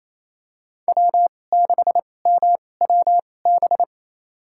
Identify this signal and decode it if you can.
Morse code, heard as a keyed tone, about 28 wpm, spaced as for 20 wpm, 700 Hz. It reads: W6MWB
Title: USA Call Signs - 5 Characters - 2x Character Spacing - 28wpm